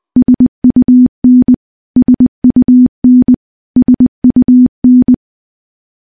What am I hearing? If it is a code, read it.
Morse code, heard as a keyed tone, about 20 wpm, 250 Hz. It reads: SUN SUN SUN
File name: morse-training-20WPM